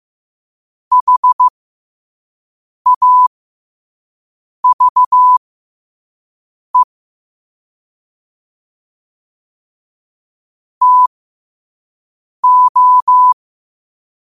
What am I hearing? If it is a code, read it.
Morse code, heard as a keyed tone, about 15 wpm, spaced as for 5 wpm, 1000 Hz. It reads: HAVE TO
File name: qotdfiles / 5wpm